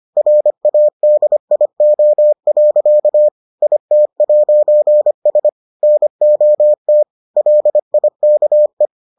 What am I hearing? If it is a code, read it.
Morse code, heard as a keyed tone, about 25 wpm, 600 Hz. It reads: RADIO. IT'S NOT LIKE